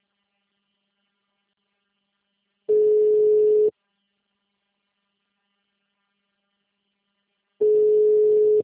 Ringback tone (Europe, 425 Hz).